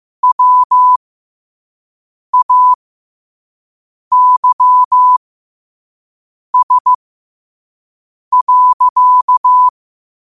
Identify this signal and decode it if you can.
Morse code, heard as a keyed tone, about 15 wpm, spaced as for 5 wpm, 1000 Hz. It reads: WAYS.
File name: qotdfiles / 5wpm